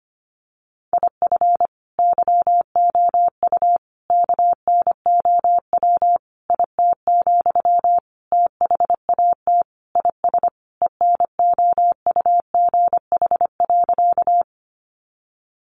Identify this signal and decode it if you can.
Morse code, heard as a keyed tone, about 25 wpm, 700 Hz. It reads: IF YOU KNOW IT, THAT IS ENOUGH.
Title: Sentences from Top 400 Words 25wpm